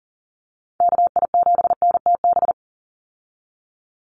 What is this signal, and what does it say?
Morse code, heard as a keyed tone, about 40 wpm, 700 Hz. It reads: KI7NTB
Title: USA Call Signs 40wpm